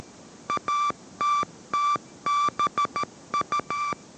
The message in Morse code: ATTBU

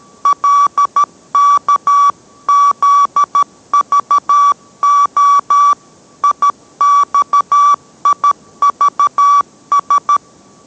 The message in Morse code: LKZVO IXIVS

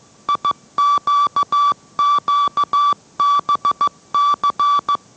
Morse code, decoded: IQQBC

